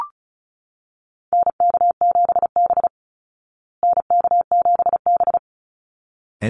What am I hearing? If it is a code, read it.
Morse code, heard as a keyed tone, about 34 wpm, 700 Hz. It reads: NK7B NK7B
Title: Call Signs Repeated 3x - 35wpm